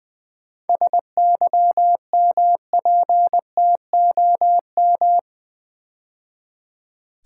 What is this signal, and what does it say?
Morse code, heard as a keyed tone, about 20 wpm, 690 Hz. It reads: SYMPTOM
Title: 7 Letter Words - First 3 Letters Given - 20wpm